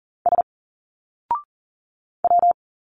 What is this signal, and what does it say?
Morse code, heard as a keyed tone, about 40 wpm, 700 Hz. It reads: S W
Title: Single Letters 40wpm